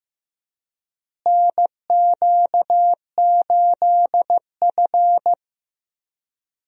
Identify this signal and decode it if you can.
Morse code, heard as a keyed tone, about 15 wpm, 700 Hz. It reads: NQ8F
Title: USA Call Signs - 4 Characters - Repeated 3x - 15wpm